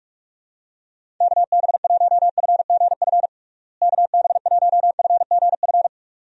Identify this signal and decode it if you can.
Morse code, heard as a keyed tone, about 42 wpm, 700 Hz. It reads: KB1FGF KB1FGF
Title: USA Call Signs - 6 Characters - Repeated 2x - 45wpm